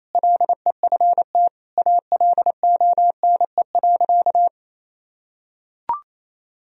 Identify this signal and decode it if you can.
Morse code, heard as a keyed tone, about 28 wpm, 700 Hz. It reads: LEFT ALONE.
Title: Sentences from Top 1000 Words - Review Part 1 - 28wpm